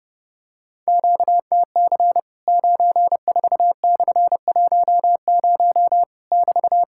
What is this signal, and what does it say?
Morse code, heard as a keyed tone, about 30 wpm, 700 Hz. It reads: QTC 94/10 <BT>
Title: Worked All Europe DX Contest - QTC Exchange - 30wpm